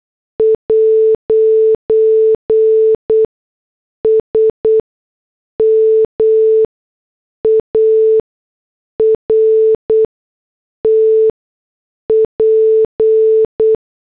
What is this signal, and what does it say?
Morse code, heard as a keyed tone, about 8 wpm, spaced as for 6 wpm, 440 Hz. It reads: 'SMART P